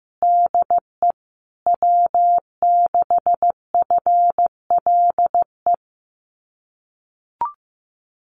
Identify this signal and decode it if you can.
Morse code, heard as a keyed tone, about 15 wpm, 700 Hz. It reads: DE W6FLE
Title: Common QSO Phrases Encore 15wpm